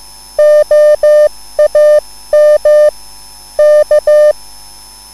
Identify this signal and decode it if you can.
Morse code, heard as a keyed tone, about 15 wpm, 600 Hz. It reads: OAM K
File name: QSL / CW lesson